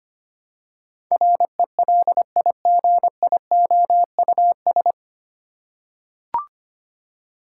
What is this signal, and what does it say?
Morse code, heard as a keyed tone, about 25 wpm, 700 Hz. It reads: RELIGIOUS
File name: Nine Letter Words - 25wpm